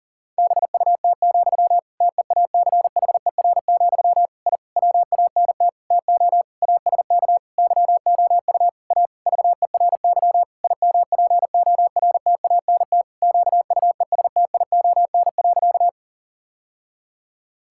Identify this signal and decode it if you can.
Morse code, heard as a keyed tone, about 40 wpm, 700 Hz. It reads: BUT, TEACHER, I WANT TO ASK YOU A VERY IMPORTANT QUESTION.
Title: Sentences from Top 1000 Words - Review Part 6 - 40wpm